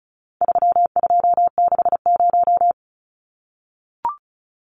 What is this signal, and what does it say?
Morse code, heard as a keyed tone, about 34 wpm, 700 Hz. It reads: 3260
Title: Numbers - 4-Digit 35wpm